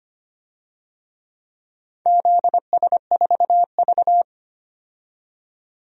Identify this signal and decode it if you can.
Morse code, heard as a keyed tone, about 25 wpm, 700 Hz. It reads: ZS4V
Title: DXCC-100 Call Signs - 4 Characters - 25wpm